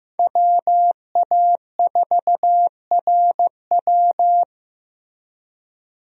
Morse code, heard as a keyed tone, about 15 wpm, 700 Hz. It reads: WA4RW
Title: USA Call Signs - 5 Characters - 15wpm